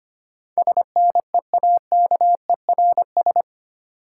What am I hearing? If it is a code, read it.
Morse code, heard as a keyed tone, about 25 wpm, 690 Hz. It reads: SNEAKERS
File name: practice / 25wpm / Easy Words with Hints - 25wpm